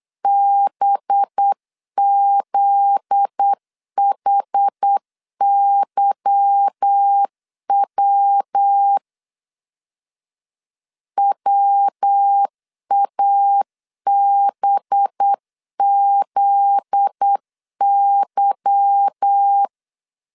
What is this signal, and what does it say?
Morse code, heard as a keyed tone, about 8 wpm, 800 Hz. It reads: BZHYW WABZY
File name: LEZIONI / 8 YJ45 / 8 Y 8wpm 3245